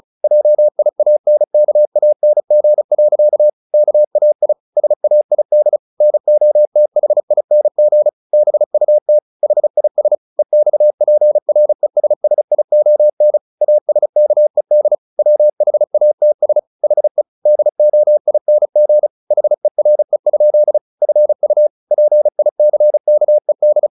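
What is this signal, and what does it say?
Morse code, heard as a keyed tone, about 35 wpm, 600 Hz. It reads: JIANKANG. KAI SAID NOTHING BUT HIS EXPRESSION ASKED WHATS HE DOING HERE? FU PICKED
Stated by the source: Morse practice at 35 wpm from CW med Gnister, sending Ken Follett's novel Never in English.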